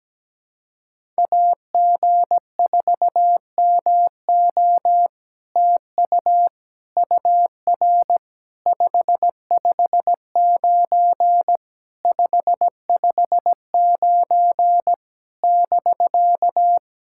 Morse code, heard as a keyed tone, about 17 wpm, 700 Hz. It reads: AG4MO TU UR 559 559 <BK>
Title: SOTA QSO Practice 17wpm